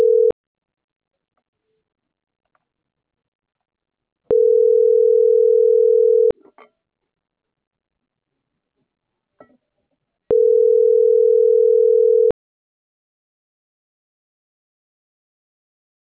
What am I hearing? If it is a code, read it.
Ringback tone (North America), 440 + 480 Hz.